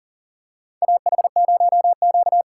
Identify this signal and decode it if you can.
Morse code, heard as a keyed tone, about 38 wpm, 700 Hz. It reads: AH0Q
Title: USA Call Signs - 4 Characters - 4x Character Spacing - 40wpm